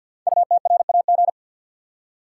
Morse code, heard as a keyed tone, about 53 wpm, 700 Hz. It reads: UTRAG
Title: Code Groups - 5 Letter:Numbers - 50wpm